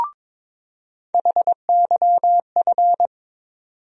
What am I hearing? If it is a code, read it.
Morse code, heard as a keyed tone, about 21 wpm, 700 Hz. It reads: HYF
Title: USA Call Sign Postfixes - 3 Letters 22wpm